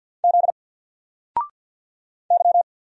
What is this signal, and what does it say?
Morse code, heard as a keyed tone, about 50 wpm, 700 Hz. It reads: Z Y
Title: Single Letter-Number 50wpm